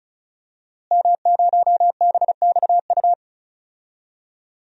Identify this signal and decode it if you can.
Morse code, heard as a keyed tone, about 36 wpm, 700 Hz. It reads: M0BXU
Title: DXCC-100 Call Signs - 5 Characters - Repeated 3x - 35wpm